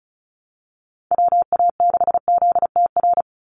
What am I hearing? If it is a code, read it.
Morse code, heard as a keyed tone, about 34 wpm, 700 Hz. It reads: WA6ZTR
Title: USA Call Signs 35wpm